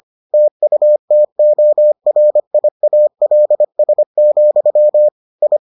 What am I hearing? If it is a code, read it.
Morse code, heard as a keyed tone, about 25 wpm, 600 Hz. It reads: TUTORIALS, I